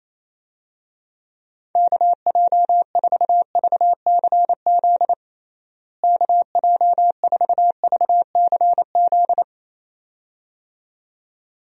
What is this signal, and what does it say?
Morse code, heard as a keyed tone, about 28 wpm, 700 Hz. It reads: KJ4VCZ KJ4VCZ
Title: USA Call Signs - 6 Characters - Repeated 2x - 28wpm